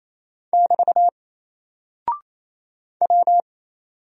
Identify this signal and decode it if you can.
Morse code, heard as a keyed tone, about 27 wpm, 700 Hz. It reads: <BT> W
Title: Single Character (Letter:Number:Punctuation:Prosign) 28wpm